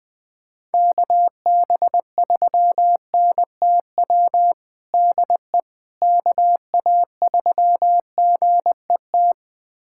Morse code, heard as a keyed tone, about 20 wpm, 700 Hz. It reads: KB3NTW DE KA3GET
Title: USA Callsign DE Callsign - 6 Characters - 20wpm